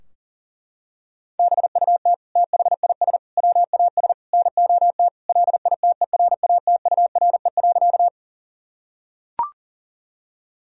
Morse code, heard as a keyed tone, about 40 wpm, 700 Hz. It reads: BUT THIS WAS NOT LITERATURE.
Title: Sentences from Top 1000 Words - Review Part 3 - 40wpm